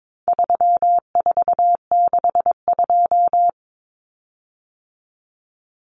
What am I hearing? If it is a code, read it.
Morse code, heard as a keyed tone, about 22 wpm, 700 Hz. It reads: 3462
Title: Common Cut Numbers - 4-Digit 22wpm